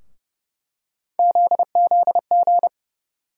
Morse code, heard as a keyed tone, about 30 wpm, 700 Hz. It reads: ZZG
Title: USA Call Sign Postfixes - 3 Letters 30wpm